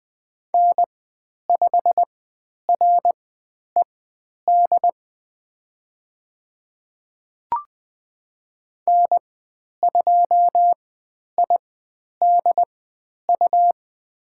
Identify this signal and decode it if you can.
Morse code, heard as a keyed tone, about 20 wpm, spaced as for 10 wpm, 700 Hz. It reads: N5RED N2IDU
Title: 102 Lesson 17g - Letter u - Call Signs 20wpm